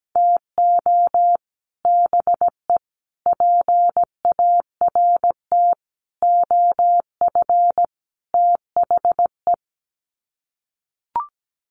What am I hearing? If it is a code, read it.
Morse code, heard as a keyed tone, about 17 wpm, 700 Hz. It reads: TO BE PART OF THE
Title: Sets of 5 Words 17wpm